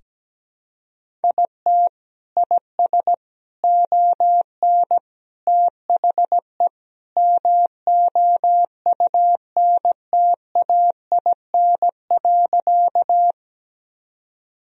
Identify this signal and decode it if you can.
Morse code, heard as a keyed tone, about 17 wpm, 700 Hz. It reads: IT IS ON THE MOUNTAIN.
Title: Sentences from Top 1000 Words - Review Part 1 - 17wpm